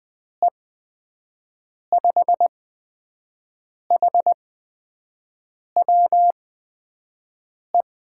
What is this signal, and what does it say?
Morse code, heard as a keyed tone, about 20 wpm, spaced as for 5 wpm, 700 Hz. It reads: E5HWE
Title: USA Call Signs - 6 Characters - 8x Character Spacing - 20wpm